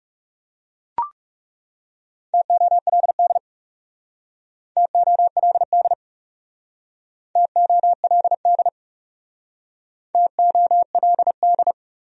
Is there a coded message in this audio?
Yes: Morse code from 2.3 to 11.7 s, heard as a keyed tone, about 38 wpm, 700 Hz. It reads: TOLD TOLD TOLD TOLD